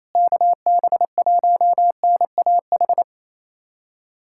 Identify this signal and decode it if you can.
Morse code, heard as a keyed tone, about 27 wpm, 700 Hz. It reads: KB1NAH
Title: USA Callsign DE Callsign 28wpm